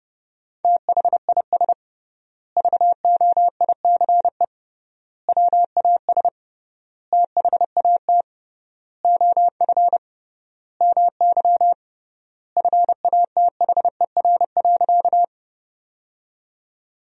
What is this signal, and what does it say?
Morse code, heard as a keyed tone, about 30 wpm, 700 Hz. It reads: THIS VOICE WAS THAT OF MY FATHER.
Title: 3x Word-Spacing - Sentences from Top 500 Words - 30wpm